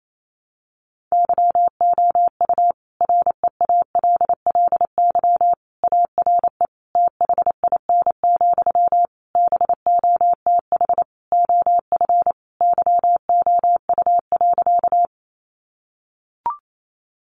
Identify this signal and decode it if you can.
Morse code, heard as a keyed tone, about 28 wpm, 700 Hz. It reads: YOU REALLY ARE THIN, BOTH OF YOU.
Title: Sentences from Top 1200 Words 28wpm